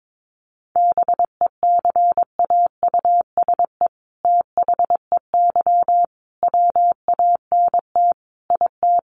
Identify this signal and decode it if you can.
Morse code, heard as a keyed tone, about 22 wpm, 700 Hz. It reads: BECAUSE THEY WANT IT
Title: Sentences from Top 800 Words 22wpm